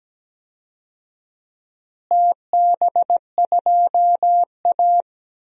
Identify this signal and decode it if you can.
Morse code, heard as a keyed tone, about 17 wpm, 700 Hz. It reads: TB2A
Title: DXCC-100 Call Signs - 4 Characters - Repeated 3x - 17wpm